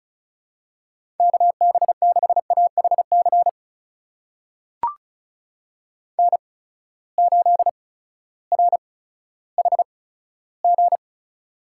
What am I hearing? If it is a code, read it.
Morse code, heard as a keyed tone, about 36 wpm, spaced as for 16 wpm, 700 Hz. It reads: KB6AHC N8RHG
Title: USA Call Signs - 8x Character Spacing - 35wpm